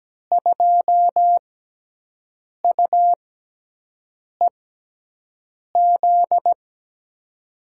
Morse code, heard as a keyed tone, about 17 wpm, spaced as for 6 wpm, 700 Hz. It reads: 2UEZ
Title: USA Call Signs - 6 Characters - 6x Character Spacing - 17wpm